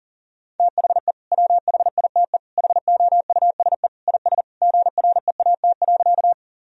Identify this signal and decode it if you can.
Morse code, heard as a keyed tone, about 40 wpm, 700 Hz. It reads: THE WHITE HOUSE IS GREAT.
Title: Sentences from Top 1000 Words - Review Part 1 - 40wpm